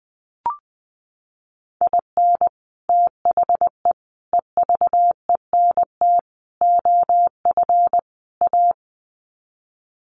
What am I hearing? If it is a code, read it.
Morse code, heard as a keyed tone, about 20 wpm, 700 Hz. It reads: IN THE EVENT OF A
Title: Sets of 5 Words Encore 20wpm